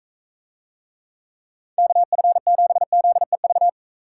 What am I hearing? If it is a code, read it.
Morse code, heard as a keyed tone, about 45 wpm, 700 Hz. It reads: KF7ZEV